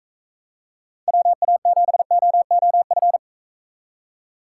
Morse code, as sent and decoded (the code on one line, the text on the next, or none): .-- .- --... --- --- ..-.
WA7OOF